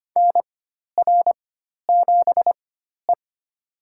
Morse code, heard as a keyed tone, about 25 wpm, spaced as for 12 wpm, 700 Hz. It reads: NR7E
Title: USA Call Signs - 4 Characters - 4x Character Spacing - 25wpm